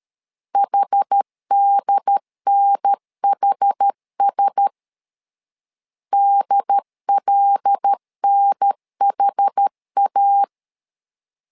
Morse code, heard as a keyed tone, about 13 wpm, 800 Hz. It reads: HDNHS DLNHA
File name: LEZIONI / 5 LBVC / 5 L 12wpm 1913